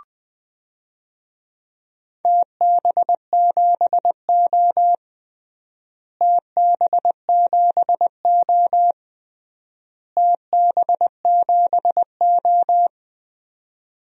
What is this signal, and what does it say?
Morse code, heard as a keyed tone, about 20 wpm, 700 Hz. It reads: TB7O TB7O TB7O
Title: DXCC-100 Call Signs - 4 Characters - Repeated 3x - 20wpm